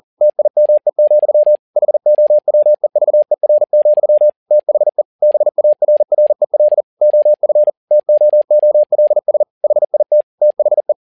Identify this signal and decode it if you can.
Morse code, heard as a keyed tone, about 40 wpm, 600 Hz. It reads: TIME, HOWEVER, THE BARREL OF TOOLS HIT THE
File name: learning center / tower trouble / tower touble 40wpm